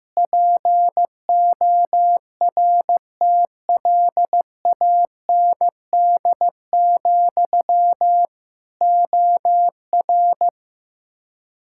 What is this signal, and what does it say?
Morse code, heard as a keyed tone, about 15 wpm, 700 Hz. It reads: PORTLAND, OR
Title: K1USN SST QSO Practice 15wpm